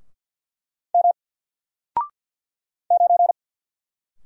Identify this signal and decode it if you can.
Morse code, heard as a keyed tone, about 48 wpm, 700 Hz. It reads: M 9
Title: Single Character (Letter:Number:Punctuation:Prosign) 50wpm